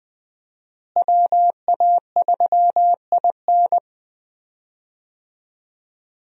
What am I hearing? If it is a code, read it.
Morse code, heard as a keyed tone, about 20 wpm, 700 Hz. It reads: WA3IN
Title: USA Call Signs - 5 Characters - Repeated 2x - 20wpm